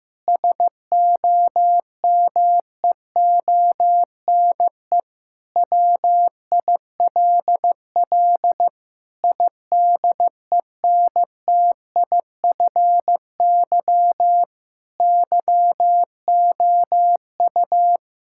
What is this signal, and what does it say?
Morse code, heard as a keyed tone, about 15 wpm, 700 Hz. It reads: SOMEONE WILL IDENTIFY YOU
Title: Sentences from Top 1000 Words - Review Part 6 - 15wpm